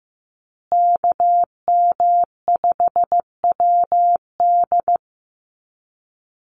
Morse code, heard as a keyed tone, about 15 wpm, 700 Hz. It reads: KM5WD
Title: USA Call Signs Repeated 2x 15wpm